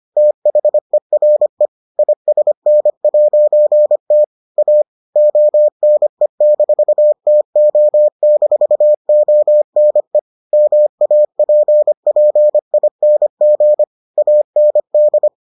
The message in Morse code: THERE ISN'T A ONE-TO-ONE MAPPING AND

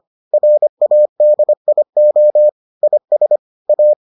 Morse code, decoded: RADIO IS A

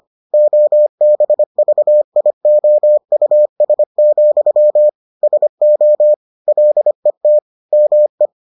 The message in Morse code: OBVIOUS, SO LET ME

